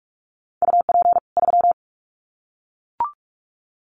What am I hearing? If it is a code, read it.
Morse code, heard as a keyed tone, about 44 wpm, 700 Hz. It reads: UP 3
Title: Common QSO Phrases 45wpm